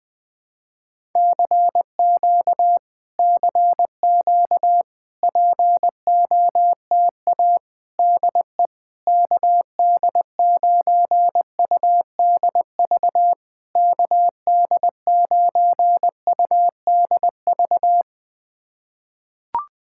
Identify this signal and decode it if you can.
Morse code, heard as a keyed tone, about 20 wpm, 690 Hz. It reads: CQ CQ POTA DE KD9UDV KD9UDV
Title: POTA QSO Practice - 20wpm